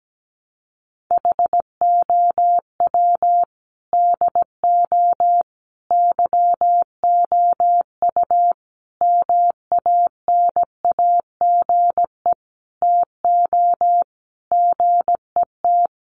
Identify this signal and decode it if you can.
Morse code, heard as a keyed tone, about 17 wpm, 700 Hz. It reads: HOW DO YOU MANAGE TO GET
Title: Sentences from Top 1400 Words 17wpm